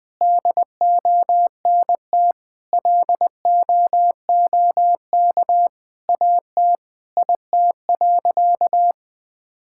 Morse code, heard as a keyed tone, about 20 wpm, 700 Hz. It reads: DONT LOOK AT IT.
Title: Sentences from Top 1000 Words - Review Part 5 - 20wpm